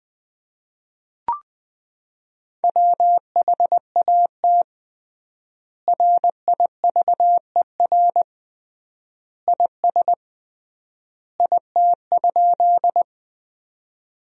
Morse code, heard as a keyed tone, about 21 wpm, 700 Hz. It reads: WHAT RIVER IS IT?
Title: 3x Word-Spacing - Sentences from Top 300 Words - 20wpm